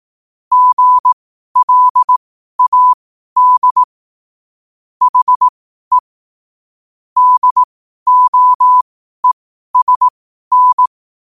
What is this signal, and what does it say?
Morse code, heard as a keyed tone, about 18 wpm, spaced as for 12 wpm, 1000 Hz. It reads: GLAD HE DOESN
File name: qotdfiles / 12wpm